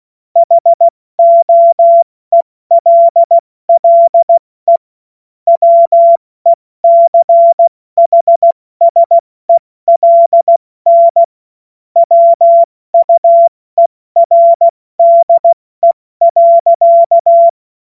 Morse code, heard as a keyed tone, about 16 wpm, 680 Hz. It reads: HOELLE WECHSELN WUERDE.
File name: Morsekurs / Witze / Witz 25. Oktober 2022 70 F80